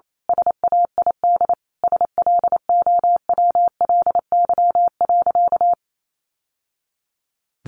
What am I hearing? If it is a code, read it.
Morse code, heard as a keyed tone, about 28 wpm, 700 Hz. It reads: SAID SLOWLY.
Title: Sentences from Top 1100 Words 28wpm